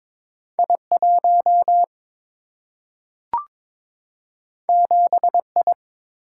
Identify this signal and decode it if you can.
Morse code, heard as a keyed tone, about 22 wpm, 700 Hz. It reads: I 1 7 I
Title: Single-Letter Single-Number 22wpm